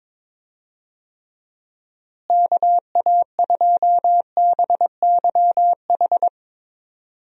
Morse code, heard as a keyed tone, about 21 wpm, 700 Hz. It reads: KA2BYH
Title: USA Call Signs - 6 Characters - Repeated 2x - 22wpm